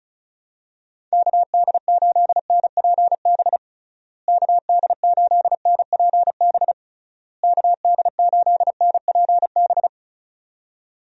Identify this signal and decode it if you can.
Morse code, heard as a keyed tone, about 36 wpm, 700 Hz. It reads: KD8NPB KD8NPB KD8NPB
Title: USA Call Signs - 6 Characters - Repeated 3x - 35wpm